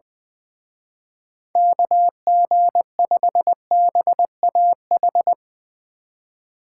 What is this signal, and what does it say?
Morse code, heard as a keyed tone, about 22 wpm, 700 Hz. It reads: KG5BAH
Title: Speed Racing - USA 2x3 Call Signs 20wpm